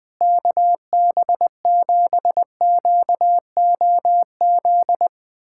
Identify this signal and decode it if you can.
Morse code, heard as a keyed tone, about 20 wpm, 700 Hz. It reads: KB7QOZ
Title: USA Callsign DE Callsign 20wpm